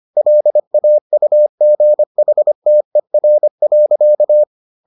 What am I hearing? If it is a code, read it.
Morse code, heard as a keyed tone, about 25 wpm, 600 Hz. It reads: LAUGHTER.